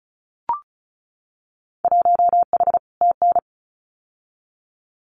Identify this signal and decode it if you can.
Morse code, heard as a keyed tone, about 36 wpm, 700 Hz. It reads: 1H TN
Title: Winter Field Day Exchanges 35wpm